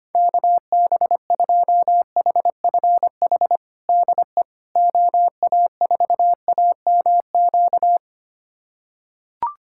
Morse code, heard as a keyed tone, about 25 wpm, 700 Hz. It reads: KB2HFH DE OA4AMQ
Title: USA Callsign DE Callsign - 6 Characters - 25wpm